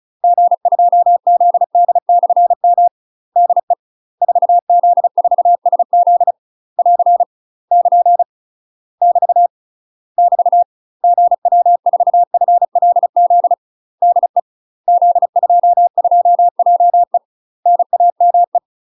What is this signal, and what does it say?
Morse code, heard as a keyed tone, about 36 wpm, 700 Hz. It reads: G2ZD/M DE 4Z4SZ <AR> <KN> <BT> <BT> GW4FLZ DE Z22JE NAME
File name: UK MASTER QSOs TRAINING / 35WPM / UK MASTER QSOs 0020